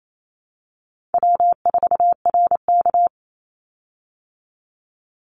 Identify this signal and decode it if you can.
Morse code, heard as a keyed tone, about 28 wpm, 700 Hz. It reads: W4RK
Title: Easy Call Signs Repeated 3x - 28wpm